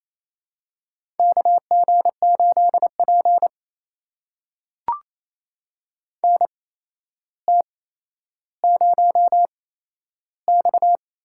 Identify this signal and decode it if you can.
Morse code, heard as a keyed tone, about 28 wpm, spaced as for 12 wpm, 700 Hz. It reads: KG8P NT0X
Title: USA Call Signs - 4 Characters - 8x Character Spacing - 28wpm